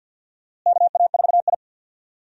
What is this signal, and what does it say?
Morse code, heard as a keyed tone, about 52 wpm, 700 Hz. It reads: KA4I
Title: USA Call Signs - 4 Characters - Repeated 3x - 50wpm